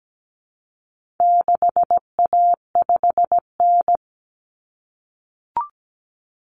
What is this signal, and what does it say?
Morse code, heard as a keyed tone, about 17 wpm, 700 Hz. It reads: 6A5N
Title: Common Cut Numbers - 4-Digit 17wpm